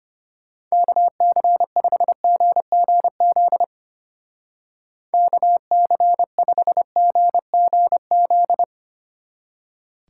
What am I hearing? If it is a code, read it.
Morse code, heard as a keyed tone, about 31 wpm, 700 Hz. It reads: KC5GGZ KC5GGZ
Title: Speed Racing - USA 2x3 Call Signs 25wpm